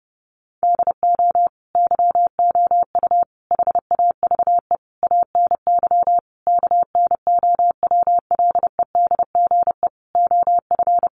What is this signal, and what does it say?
Morse code, heard as a keyed tone, about 30 wpm, 700 Hz. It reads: DO YOU HAVE ANY KNOWLEDGE OF
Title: Sentences from Top 600 Words 30wpm